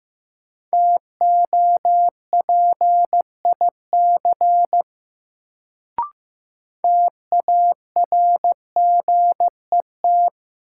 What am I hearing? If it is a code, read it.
Morse code, heard as a keyed tone, about 15 wpm, 700 Hz. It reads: TOPIC TARGET
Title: Sentences from Top 1200 Words - New Word Drill Alt - 15wpm